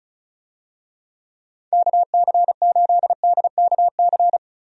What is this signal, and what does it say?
Morse code, heard as a keyed tone, about 36 wpm, 700 Hz. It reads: KC8DKC
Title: USA Call Signs - 6 Characters - Repeated 3x - 35wpm